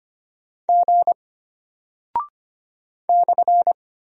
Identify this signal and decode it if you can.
Morse code, heard as a keyed tone, about 25 wpm, 700 Hz. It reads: G /
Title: Single Character (Letter:Number:Punctuation:Prosign) Alt 25wpm